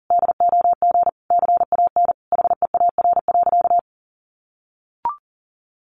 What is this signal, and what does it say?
Morse code, heard as a keyed tone, about 40 wpm, 700 Hz. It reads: DOG CAN HEAR.
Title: A Sound Of Thunder 40wpm